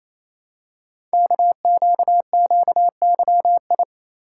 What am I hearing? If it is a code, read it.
Morse code, heard as a keyed tone, about 28 wpm, 700 Hz. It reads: KQQYI
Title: Code Groups - 5 Letter:Numbers - 28wpm